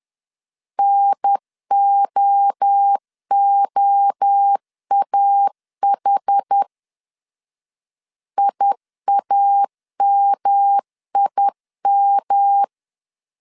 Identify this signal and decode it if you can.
Morse code, heard as a keyed tone, about 10 wpm, 800 Hz. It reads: NOOAH IAMIM
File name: CW / DET / 1 3 1500C S C 10wpm 3335